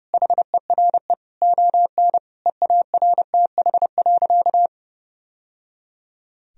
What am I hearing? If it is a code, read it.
Morse code, heard as a keyed tone, about 30 wpm, 700 Hz. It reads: HERE ON EARTH.
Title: Sentences from Top 1000 Words - Review Part 2 - 30wpm